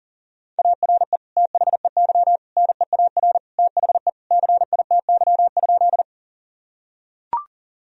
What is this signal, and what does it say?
Morse code, heard as a keyed tone, about 40 wpm, 700 Hz. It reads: ARE THEY NEAR THE CITY?
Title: Sentences from Top 1000 Words - Review Part 1 - 40wpm